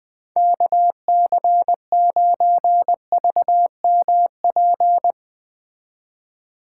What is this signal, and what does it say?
Morse code, heard as a keyed tone, about 20 wpm, 700 Hz. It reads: KC9VMP
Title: Speed Racing - USA 2x3 Call Signs 17wpm